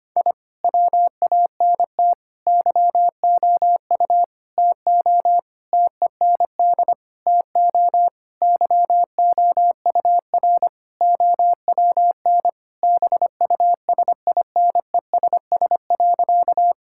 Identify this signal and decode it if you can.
Morse code, heard as a keyed tone, about 25 wpm, 700 Hz. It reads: I WANT YOU TO TEND TO YOUR OWN BUSINESS.
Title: Sentences from Top 1000 Words - Review Part 5 - 25wpm